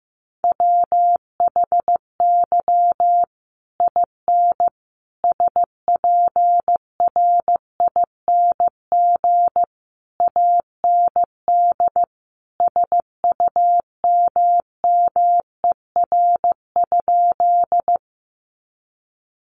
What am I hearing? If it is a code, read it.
Morse code, heard as a keyed tone, about 15 wpm, 700 Hz. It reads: WHY IN SPRING AND SUMMER?
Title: Sentences from Top 1200 Words 15wpm